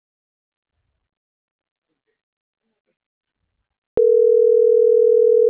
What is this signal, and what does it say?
Ringback tone (North America), 440 + 480 Hz.